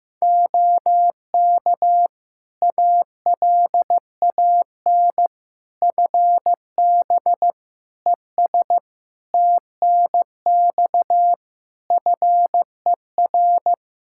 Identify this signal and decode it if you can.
Morse code, heard as a keyed tone, about 15 wpm, 700 Hz. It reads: OK ALAN FB ES TNX FER
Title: Practice QSO - Exchange 2 - Info - 15wpm